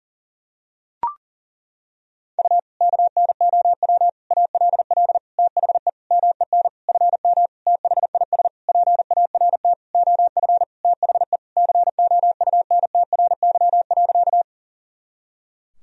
Morse code, heard as a keyed tone, about 37 wpm, 700 Hz. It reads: U KNOW ALL THE MEN FM THIS PART OF THE COUNTRY.
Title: Sentences from Top 300 Words and Abbreviations 40wpm